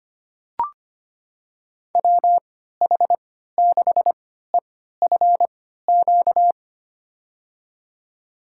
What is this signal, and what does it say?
Morse code, heard as a keyed tone, about 25 wpm, spaced as for 14 wpm, 700 Hz. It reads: WH6EFQ
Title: USA Call Signs - 6 Characters - 3x Character Spacing - 25wpm